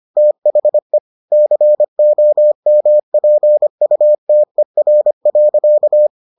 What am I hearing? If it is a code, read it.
Morse code, heard as a keyed tone, about 25 wpm, 600 Hz. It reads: THE COMPUTER.